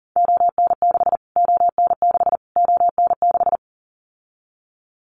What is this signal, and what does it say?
Morse code, heard as a keyed tone, about 40 wpm, 700 Hz. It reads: ON6 ON6 ON6
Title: Callsign Prefixes for Top 25 DXCC - Repeated 3x - 40wpm